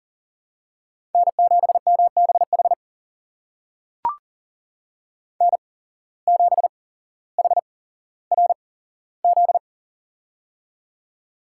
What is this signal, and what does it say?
Morse code, heard as a keyed tone, about 40 wpm, spaced as for 17 wpm, 700 Hz. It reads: N7MBH N7HRZ
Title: USA Call Signs - 5 Characters - 8x Character Spacing - 40wpm